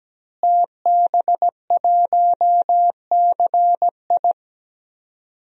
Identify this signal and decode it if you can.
Morse code, heard as a keyed tone, about 17 wpm, 700 Hz. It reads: TB1CI
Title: DXCC-100 Call Signs - 5 Characters - Repeated 3x - 17wpm